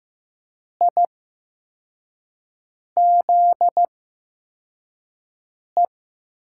Morse code, heard as a keyed tone, about 15 wpm, spaced as for 4 wpm, 700 Hz. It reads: IZE
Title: USA Call Signs - 5 Characters - 8x Character Spacing - 15wpm